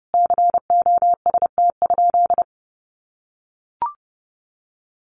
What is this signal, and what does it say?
Morse code, heard as a keyed tone, about 30 wpm, 700 Hz. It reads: COST?
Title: Sentences from Top 1400 Words 30wpm